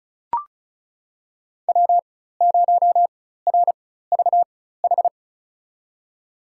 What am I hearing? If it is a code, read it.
Morse code, heard as a keyed tone, about 36 wpm, spaced as for 16 wpm, 700 Hz. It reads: W0RVH
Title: USA Call Signs - 5 Characters - 4x Character Spacing - 35wpm